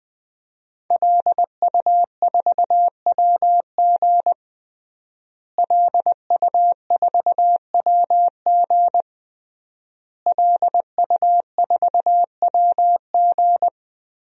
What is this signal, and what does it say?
Morse code, heard as a keyed tone, about 20 wpm, 700 Hz. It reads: LU4WG LU4WG LU4WG
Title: DXCC-100 Call Signs - 5 Characters - Repeated 3x - 20wpm